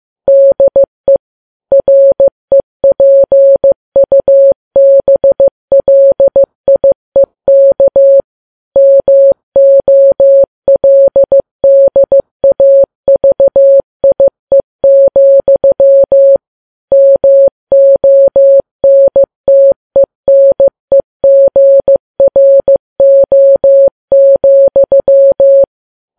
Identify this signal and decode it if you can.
Morse code, heard as a keyed tone, about 15 wpm, 550 Hz. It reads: DE REPUBLIEK MOLDAVIE, MONTENEGRO,
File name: Files Web tls / Morse on9fv / morse15kps